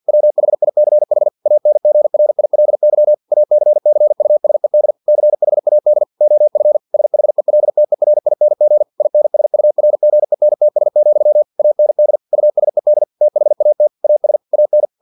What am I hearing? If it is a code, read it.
Morse code, heard as a keyed tone, about 50 wpm, 600 Hz. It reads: WHICH ANGRILY ACCUSED CHAD OF SHELTERING INSURGENTS, AND USED THAT AS AN
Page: Never 49wpm